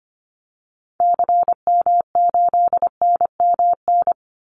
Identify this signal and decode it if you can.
Morse code, heard as a keyed tone, about 25 wpm, 700 Hz. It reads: CM8NMN
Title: Call Signs Repeated 3x - 25wpm